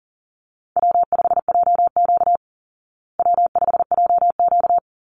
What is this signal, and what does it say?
Morse code, heard as a keyed tone, about 38 wpm, 700 Hz. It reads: W5JQ W5JQ
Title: Easy Call Signs Repeated 2x - 40wpm